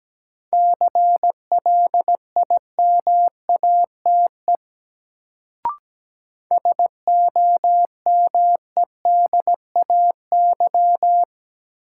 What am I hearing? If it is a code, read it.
Morse code, heard as a keyed tone, about 17 wpm, 700 Hz. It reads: CLIMATE SOMEDAY
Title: Sentences from Top 1600 Words - New Word Drill Alt - 17wpm